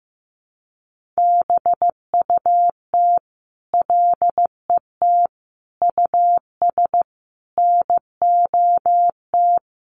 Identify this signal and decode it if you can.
Morse code, heard as a keyed tone, about 15 wpm, 700 Hz. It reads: BUT LET US NOT
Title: Sentences from Top 1500 Words 15wpm